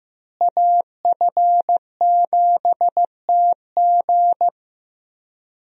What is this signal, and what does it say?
Morse code, heard as a keyed tone, about 15 wpm, 700 Hz. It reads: AF7TG
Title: USA Call Signs - 5 Characters - Repeated 3x - 15wpm